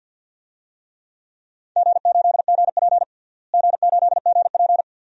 Morse code, heard as a keyed tone, about 48 wpm, 700 Hz. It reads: G8GP G8GP
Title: DXCC-100 Call Signs - 4 Characters - Repeated 2x - 50wpm